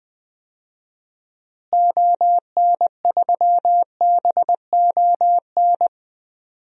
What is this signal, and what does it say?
Morse code, heard as a keyed tone, about 20 wpm, 700 Hz. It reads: ON3BON
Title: DXCC-100 Call Signs - 6 Characters - Repeated 2x - 20wpm